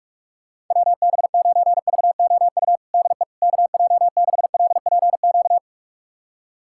Morse code, heard as a keyed tone, about 43 wpm, 700 Hz. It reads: WB9VOU DE KJ6LPQ
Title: USA Callsign DE Callsign - 6 Characters - 45wpm